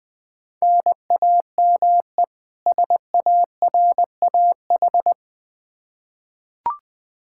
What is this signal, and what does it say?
Morse code, heard as a keyed tone, about 21 wpm, 700 Hz. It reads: NAME SARAH
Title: K1USN SST QSO Practice 20wpm